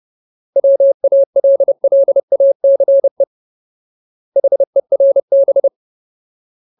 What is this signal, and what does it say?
Morse code, heard as a keyed tone, about 30 wpm, 550 Hz. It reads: WALLACE HERB
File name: Ultimate Word List / 30WPM Ultimate